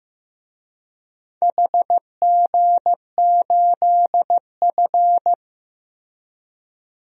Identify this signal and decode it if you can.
Morse code, heard as a keyed tone, about 15 wpm, 700 Hz. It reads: HG8F
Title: DXCC-100 Call Signs - 4 Characters - Repeated 2x - 15wpm